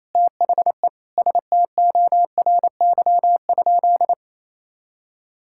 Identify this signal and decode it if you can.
Morse code, heard as a keyed tone, about 28 wpm, 700 Hz. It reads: THE STORY?
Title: Sentences from Top 1000 Words - Review Part 5 - 28wpm